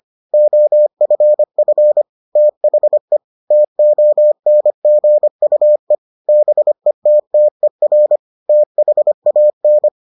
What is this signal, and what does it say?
Morse code, heard as a keyed tone, about 25 wpm, 600 Hz. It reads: OFF THE TONGUE BETTER THAN